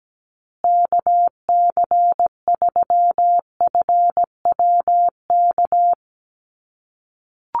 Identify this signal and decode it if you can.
Morse code, heard as a keyed tone, about 17 wpm, 700 Hz. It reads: KC3FWK
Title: USA Call Signs Repeated 2x 17wpm